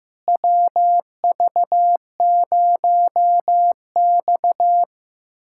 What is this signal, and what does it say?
Morse code, heard as a keyed tone, about 15 wpm, 700 Hz. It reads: WV0X
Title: USA Callsign DE Callsign - 4 Characters - 15wpm